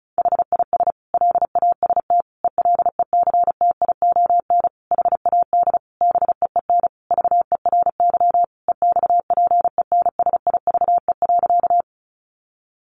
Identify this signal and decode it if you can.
Morse code, heard as a keyed tone, about 34 wpm, 700 Hz. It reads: HIS LAST ELECTION HAD BEEN VERY EXPENSIVE.
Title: Sentences from Top 1500 Words 35wpm